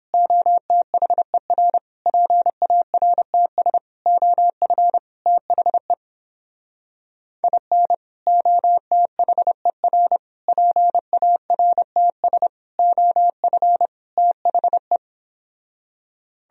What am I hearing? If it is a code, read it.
Morse code, heard as a keyed tone, about 31 wpm, 700 Hz. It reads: OTHER PARTS OF THE IN OTHER PARTS OF THE